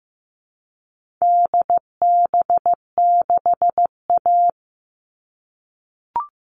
Morse code, heard as a keyed tone, about 15 wpm, 700 Hz. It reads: DB6A
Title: All Cut Numbers - 4-Digit 15wpm